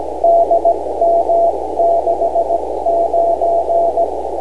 Morse code, heard as a keyed tone, about 17 wpm, spaced as for 15 wpm, 680 Hz, with noise. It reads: DM69
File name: beacons / k0ec-b